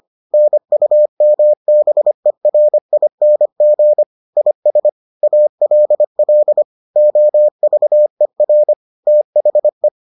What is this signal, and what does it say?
Morse code, heard as a keyed tone, about 25 wpm, 600 Hz. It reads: NUMBERING IS ALL OVER THE